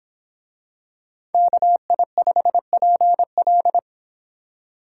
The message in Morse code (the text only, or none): KI5PL